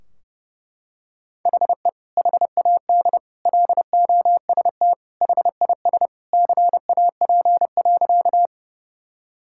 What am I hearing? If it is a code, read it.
Morse code, heard as a keyed tone, about 32 wpm, 700 Hz. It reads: HE HAD LOST HIS CAP.
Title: Sentences from Top 1600 Words 30wpm